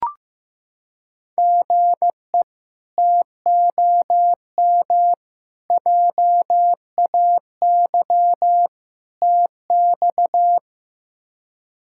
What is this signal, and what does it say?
Morse code, heard as a keyed tone, about 15 wpm, 700 Hz. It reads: GE TOM JAY TX
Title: K1USN SST QSO Practice - 15wpm